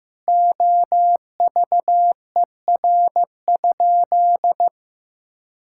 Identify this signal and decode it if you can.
Morse code, heard as a keyed tone, about 15 wpm, 700 Hz. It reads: OVER?
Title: Sentences from Top 1000 Words - Review Part 6 - 15wpm